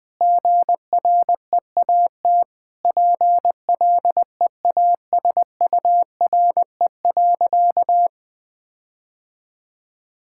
Morse code, heard as a keyed tone, about 20 wpm, 700 Hz. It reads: GREAT PLEASURE.
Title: Sentences from Top 1600 Words 20wpm